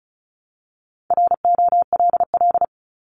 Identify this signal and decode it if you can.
Morse code, heard as a keyed tone, about 36 wpm, 690 Hz. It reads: ROLL
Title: Multi-letter Phonemes 35wpm